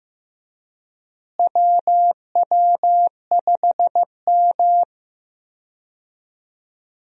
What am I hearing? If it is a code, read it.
Morse code, heard as a keyed tone, about 15 wpm, 700 Hz. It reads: WW5M
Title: USA Call Signs - 4 Characters - Repeated 2x - 15wpm